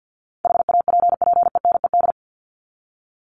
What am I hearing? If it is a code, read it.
Morse code, heard as a keyed tone, about 50 wpm, 700 Hz. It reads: HAPPENED
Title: ED Suffix - Top 100 Words with Base Words 50wpm